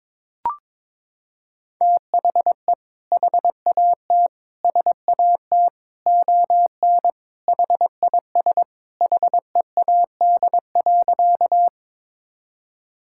Morse code, heard as a keyed tone, about 21 wpm, 700 Hz. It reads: THE HAT SAT ON HIS HEAD.
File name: practice / 22wpm / Sentences with 2-Word Phrases - 22wpm